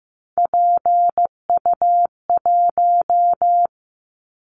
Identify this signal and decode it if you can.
Morse code, heard as a keyed tone, about 15 wpm, 700 Hz. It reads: PU1
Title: Callsign Prefixes for Top 100 DXCC - 15wpm